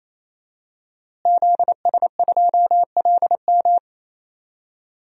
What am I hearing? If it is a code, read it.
Morse code, heard as a keyed tone, about 27 wpm, 700 Hz. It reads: ZS2LM
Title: DXCC-100 Call Signs - 5 Characters - Repeated 3x - 28wpm